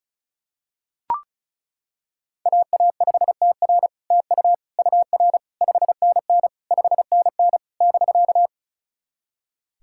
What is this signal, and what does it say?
Morse code, heard as a keyed tone, about 36 wpm, 700 Hz. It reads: AA5TR TU UR 5NN 5NN <BK>
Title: SOTA QSO Practice 35wpm